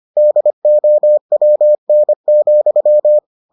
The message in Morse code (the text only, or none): DOWN,